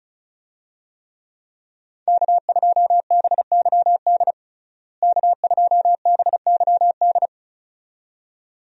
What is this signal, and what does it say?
Morse code, heard as a keyed tone, about 36 wpm, 700 Hz. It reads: K2BYD K2BYD
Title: USA Call Signs - 5 Characters - Repeated 2x - 35wpm